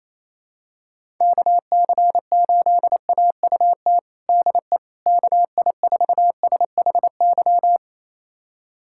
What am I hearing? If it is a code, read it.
Morse code, heard as a keyed tone, about 28 wpm, 700 Hz. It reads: KC8AUT DE KI4SHY
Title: USA Callsign DE Callsign - 6 Characters - 28wpm